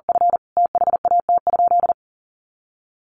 Morse code, heard as a keyed tone, about 40 wpm, 700 Hz. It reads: F THAT?
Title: Sentences from Top 300 Words 40wpm